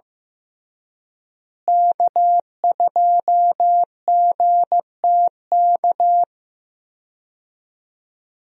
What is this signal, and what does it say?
Morse code, heard as a keyed tone, about 15 wpm, 700 Hz. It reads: K2GTK
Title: USA Call Signs - 5 Characters - Repeated 3x - 15wpm